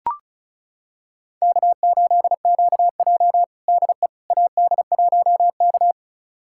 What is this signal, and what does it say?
Morse code, heard as a keyed tone, about 34 wpm, 700 Hz. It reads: K8QJ DE AD1K
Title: USA Callsign DE Callsign - 4 Characters - 35wpm